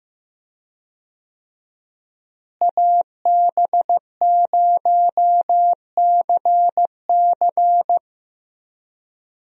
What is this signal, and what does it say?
Morse code, heard as a keyed tone, about 15 wpm, 700 Hz. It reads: AB0CC
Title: USA Call Signs - 5 Characters - Repeated 2x - 15wpm